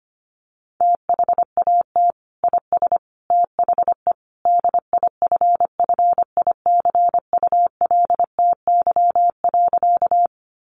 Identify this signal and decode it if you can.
Morse code, heard as a keyed tone, about 25 wpm, 700 Hz. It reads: THAT IS THE DIFFICULTY.
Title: Sentences from Top 1500 Words 25wpm